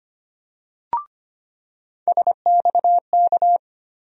Morse code, heard as a keyed tone, about 25 wpm, 700 Hz. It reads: SXK
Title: USA Call Sign Postfixes - 3 Letters 25wpm